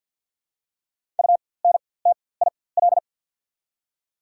Morse code, heard as a keyed tone, about 50 wpm, spaced as for 23 wpm, 690 Hz. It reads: UNTIL
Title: 4x Character Spacing - Top 300 Words - 50wpm